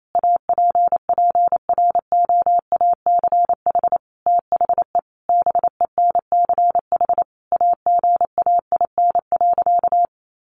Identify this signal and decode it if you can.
Morse code, heard as a keyed tone, about 28 wpm, 700 Hz. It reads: APPROACH THE BENCH AGAIN.
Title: Sentences from Top 1400 Words 28wpm